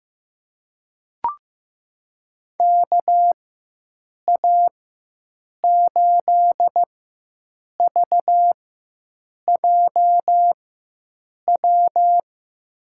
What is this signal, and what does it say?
Morse code, heard as a keyed tone, about 15 wpm, spaced as for 7 wpm, 700 Hz. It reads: KA8VJW
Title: USA Call Signs - 6 Characters - 4x Character Spacing - 15wpm